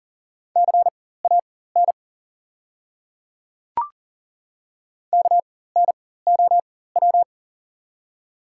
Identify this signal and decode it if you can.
Morse code, heard as a keyed tone, about 40 wpm, spaced as for 19 wpm, 700 Hz. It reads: CAN KNOW
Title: 4x Character Spacing - Top 100 Words - 40wpm